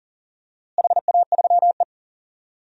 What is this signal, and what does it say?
Morse code, heard as a keyed tone, about 40 wpm, 700 Hz. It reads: HA3E
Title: DXCC-100 Call Signs - 4 Characters - Repeated 2x - 40wpm